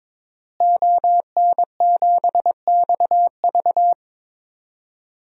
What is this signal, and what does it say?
Morse code, heard as a keyed tone, about 22 wpm, 700 Hz. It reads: ON7XV
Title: DXCC-100 Call Signs - 5 Characters - Repeated 2x - 22wpm